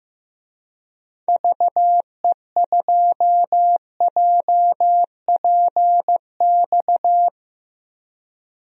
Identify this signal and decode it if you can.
Morse code, heard as a keyed tone, about 15 wpm, 700 Hz. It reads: VE2JPX
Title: DXCC-100 Call Signs - 6 Characters - Repeated 2x - 15wpm